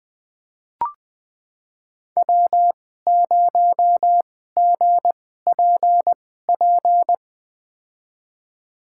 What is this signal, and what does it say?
Morse code, heard as a keyed tone, about 20 wpm, spaced as for 15 wpm, 700 Hz. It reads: W0GPP
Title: USA Call Signs - 2x Character Spacing - 20wpm